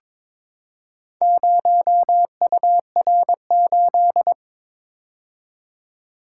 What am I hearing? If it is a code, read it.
Morse code, heard as a keyed tone, about 22 wpm, 700 Hz. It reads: 0UR8
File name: practice / 22wpm / Four Letter-Number Combinations - 22wpm